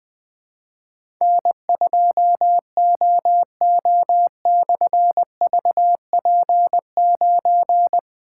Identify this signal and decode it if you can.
Morse code, heard as a keyed tone, about 20 wpm, 700 Hz. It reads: N2OO/VP9
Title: Difficult Call Signs 20wpm